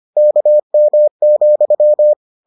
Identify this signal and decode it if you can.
Morse code, heard as a keyed tone, about 25 wpm, 600 Hz. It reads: KM,